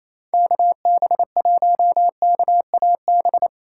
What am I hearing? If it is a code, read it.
Morse code, heard as a keyed tone, about 28 wpm, 700 Hz. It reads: KB1KAB
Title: USA Call Signs - 6 Characters - 3x Character Spacing - 28wpm